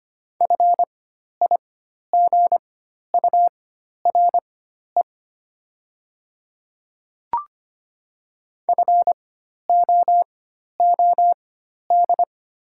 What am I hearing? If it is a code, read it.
Morse code, heard as a keyed tone, about 25 wpm, spaced as for 12 wpm, 700 Hz. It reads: FIGURE FOOD
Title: 4x Character Spacing - Top 500 Words - 25wpm